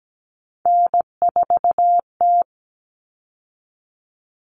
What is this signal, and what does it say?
Morse code, heard as a keyed tone, about 17 wpm, 700 Hz. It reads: N4T
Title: All Cut Numbers - 3-Digit 17wpm